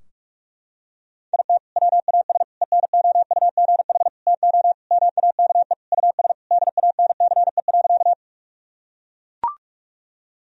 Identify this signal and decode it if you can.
Morse code, heard as a keyed tone, about 41 wpm, 700 Hz. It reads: IT WAS ENOUGH TO MAKE US DANCE.
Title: Sentences from Top 1000 Words - Review Part 4 - 45wpm